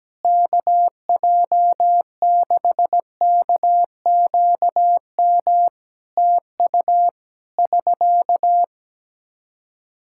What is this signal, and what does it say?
Morse code, heard as a keyed tone, about 17 wpm, 700 Hz. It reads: KJ6KQM TU <SK>
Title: Practice QSO - Exchange 4 - Ending - 17wpm